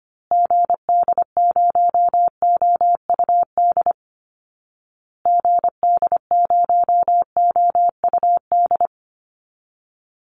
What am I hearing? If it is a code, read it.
Morse code, heard as a keyed tone, about 25 wpm, 700 Hz. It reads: GD0OUD GD0OUD
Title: Call Signs Repeated 2x - 25wpm